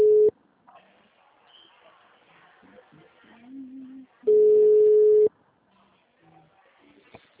Ringback tone (Europe, 425 Hz).